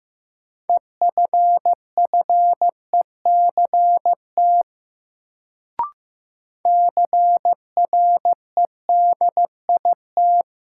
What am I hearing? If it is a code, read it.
Morse code, heard as a keyed tone, about 15 wpm, 700 Hz. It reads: EFFECT CREDIT
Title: Sentences from Top 1000 Words - New Word Drill - 15wpm